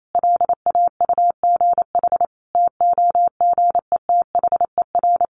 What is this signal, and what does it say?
Morse code, heard as a keyed tone, about 28 wpm, 700 Hz. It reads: LAUGH TOGETHER
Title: Sentences from Top 1500 Words 28wpm